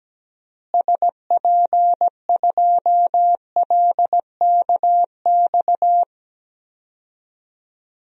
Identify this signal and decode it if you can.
Morse code, heard as a keyed tone, about 17 wpm, 700 Hz. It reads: SP2LKX
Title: DXCC-100 Call Signs - 6 Characters - Repeated 3x - 17wpm